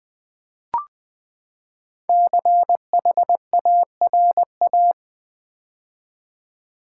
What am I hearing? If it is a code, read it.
Morse code, heard as a keyed tone, about 20 wpm, 700 Hz. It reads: CHARA
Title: Build A Word - 5 Letters At A Time - 12-Letter Words - 20wpm